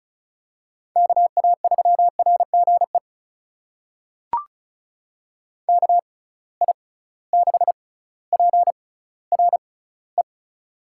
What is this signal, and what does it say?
Morse code, heard as a keyed tone, about 36 wpm, spaced as for 18 wpm, 700 Hz. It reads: KA3RGE KI6PRE
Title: USA Call Signs - 6 Characters - 6x Character Spacing - 35wpm